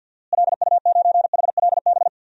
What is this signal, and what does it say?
Morse code, heard as a keyed tone, about 51 wpm, 700 Hz. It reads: LU9HLB
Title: DXCC-100 Call Signs - 6 Characters - Repeated 2x - 50wpm